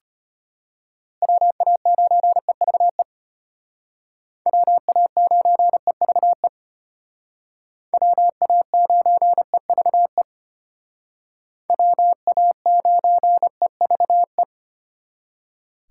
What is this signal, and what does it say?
Morse code, heard as a keyed tone, about 33 wpm, 700 Hz. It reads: WA9EVE WA9EVE WA9EVE WA9EVE